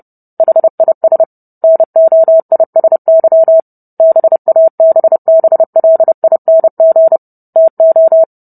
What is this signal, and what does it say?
Morse code, heard as a keyed tone, about 30 wpm, 650 Hz. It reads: HIS NOISY BABBLING TO